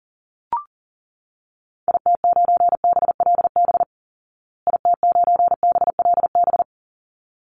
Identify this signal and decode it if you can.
Morse code, heard as a keyed tone, about 40 wpm, 700 Hz. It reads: IT9BLB IT9BLB
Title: Call Signs Repeated 3x - 40wpm